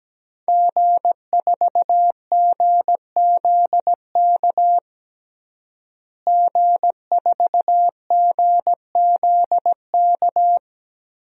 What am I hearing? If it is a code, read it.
Morse code, heard as a keyed tone, about 17 wpm, 700 Hz. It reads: G4GZK G4GZK
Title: DXCC-100 Call Signs - 5 Characters - Repeated 2x - 17wpm